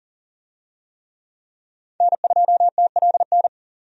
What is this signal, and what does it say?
Morse code, heard as a keyed tone, about 40 wpm, 700 Hz. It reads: N2TLN
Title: USA Call Signs - 5 Characters - 40wpm